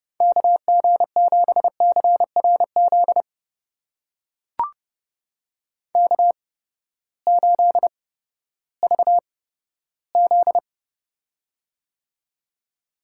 Morse code, heard as a keyed tone, about 32 wpm, spaced as for 15 wpm, 700 Hz. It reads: KG7CRZ K8VZ
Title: USA Call Signs - 8x Character Spacing - 30wpm